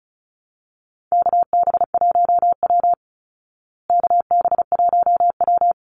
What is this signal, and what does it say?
Morse code, heard as a keyed tone, about 34 wpm, 700 Hz. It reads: KB1W KB1W
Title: Call Signs Repeated 3x - 35wpm